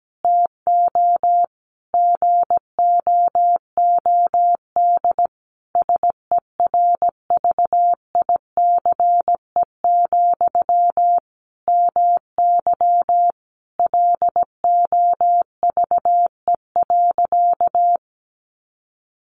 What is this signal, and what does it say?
Morse code, heard as a keyed tone, about 17 wpm, 700 Hz. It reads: TO GOOD SERVICE, MY LOVE.
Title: Sentences from Top 1100 Words 17wpm